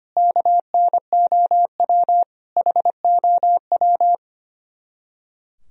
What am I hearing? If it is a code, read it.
Morse code, heard as a keyed tone, about 32 wpm, 700 Hz. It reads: KNOW HOW